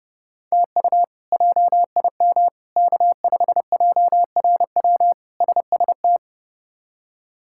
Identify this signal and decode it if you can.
Morse code, heard as a keyed tone, about 30 wpm, 700 Hz. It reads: TU JIM K5JRW SST
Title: K1USN SST QSO Practice - 30wpm